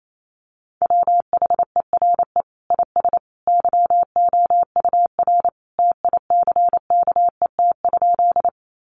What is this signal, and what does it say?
Morse code, heard as a keyed tone, about 28 wpm, 700 Hz. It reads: WHERE IS YOUR TICKET?
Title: Sentences from Top 1400 Words 28wpm